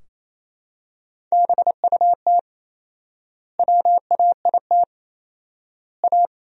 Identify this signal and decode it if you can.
Morse code, heard as a keyed tone, about 27 wpm, 700 Hz. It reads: BUT WAIT A
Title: 4x Word-Spacing - Sentences from Top 500 Words - 28wpm